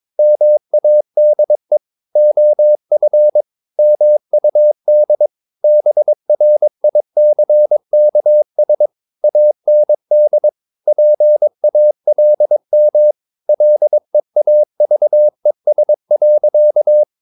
Morse code, heard as a keyed tone, about 22 wpm, 600 Hz. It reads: MADE OF MUD BRICKS AND PALM LEAVES.
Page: Romanen Never af Ken Follet på engelsk.